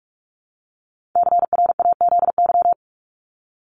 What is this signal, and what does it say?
Morse code, heard as a keyed tone, about 44 wpm, 700 Hz. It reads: CRAZY
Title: Top 1501-1750 Words 45wpm